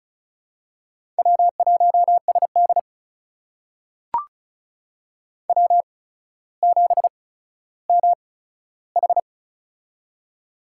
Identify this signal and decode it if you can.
Morse code, heard as a keyed tone, about 36 wpm, spaced as for 15 wpm, 700 Hz. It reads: W1SD W7MH
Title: USA Call Signs - 4 Characters - 8x Character Spacing - 35wpm